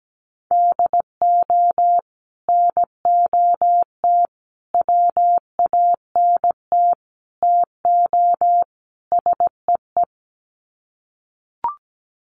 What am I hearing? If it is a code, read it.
Morse code, heard as a keyed tone, about 17 wpm, 700 Hz. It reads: DO NOT WANT TO SEE
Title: Sets of 5 Words 17wpm